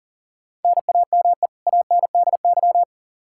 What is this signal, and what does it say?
Morse code, heard as a keyed tone, about 40 wpm, 700 Hz. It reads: NAME ANDY
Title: Practice QSO - Exchange 1 - Beginning - 40wpm